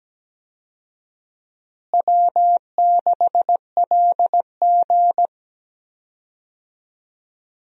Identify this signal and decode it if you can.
Morse code, heard as a keyed tone, about 17 wpm, 700 Hz. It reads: W6LG
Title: USA Call Signs - 4 Characters - Repeated 3x - 17wpm